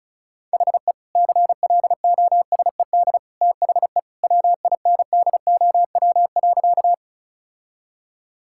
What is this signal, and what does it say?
Morse code, heard as a keyed tone, about 36 wpm, 700 Hz. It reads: HE CLOSED THE WINDOW.
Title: Sentences from Top 1600 Words 35wpm